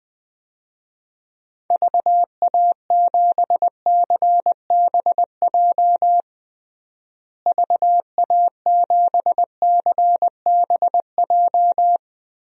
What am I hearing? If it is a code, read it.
Morse code, heard as a keyed tone, about 20 wpm, 700 Hz. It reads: VA7CBJ VA7CBJ
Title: DXCC-100 Call Signs - 6 Characters - Repeated 3x - 20wpm